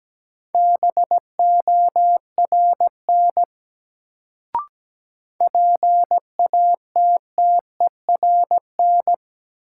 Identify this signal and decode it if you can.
Morse code, heard as a keyed tone, about 17 wpm, 700 Hz. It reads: BORN PATTERN
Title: Sentences from Top 1000 Words - New Word Drill - 17wpm